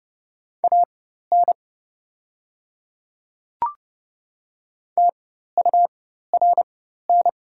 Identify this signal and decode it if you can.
Morse code, heard as a keyed tone, about 30 wpm, spaced as for 14 wpm, 700 Hz. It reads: AN TURN
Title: 4x Character Spacing - Top 200 Words - 30wpm